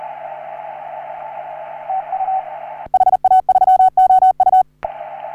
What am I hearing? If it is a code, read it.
Morse code, heard as a keyed tone, about 38 wpm, 750 Hz, with radio noise. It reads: TU HA3OU
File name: CW / HA3OU RHOBCW REC